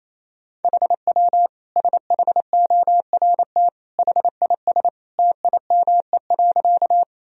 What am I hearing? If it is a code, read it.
Morse code, heard as a keyed tone, about 28 wpm, 700 Hz. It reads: HW SHORT HIS TIME.
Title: Sentences from Top 400 Words and Abbreviations 28wpm